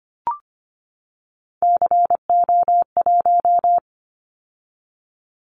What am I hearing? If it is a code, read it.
Morse code, heard as a keyed tone, about 25 wpm, 700 Hz. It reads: CO1
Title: Callsign Prefixes for Top 100 DXCC - 25wpm